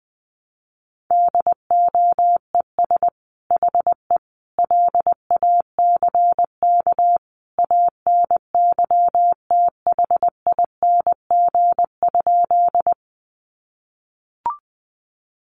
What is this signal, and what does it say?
Morse code, heard as a keyed tone, about 20 wpm, 700 Hz. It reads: DOES HE LACK ANYTHING?
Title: Sentences from Top 1100 Words 20wpm